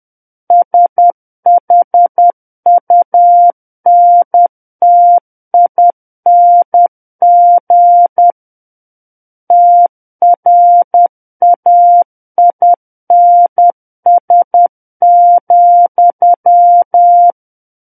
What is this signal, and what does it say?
Morse code, heard as a keyed tone, about 10 wpm, 700 Hz. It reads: SHUNTING TRAINS,
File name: Book One / 01-Chapter 1 - 10 WPM